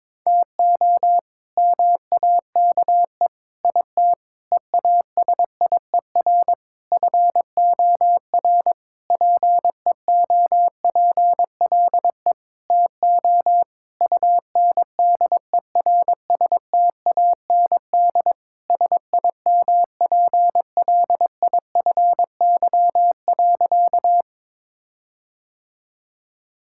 Morse code, heard as a keyed tone, about 22 wpm, 700 Hz. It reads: TO MAKE IT EASIER FOR PEOPLE TO UNDERSTAND SIMPLIFY.
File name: Sentences with 7-Word Phrases - 22wpm